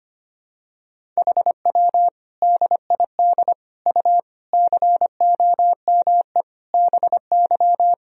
Morse code, heard as a keyed tone, about 25 wpm, 700 Hz. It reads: HW DID U COME BY
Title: Sentences from Top 100 Words and Abbreviations 25wpm